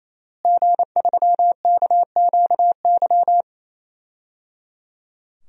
Morse code, heard as a keyed tone, about 28 wpm, 700 Hz. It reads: G3KQY
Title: DXCC-100 Call Signs - 5 Characters - Repeated 3x - 28wpm